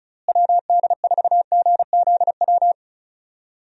Morse code, heard as a keyed tone, about 34 wpm, 700 Hz. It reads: WD4GZW
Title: USA Call Signs - 6 Characters - Repeated 3x - 35wpm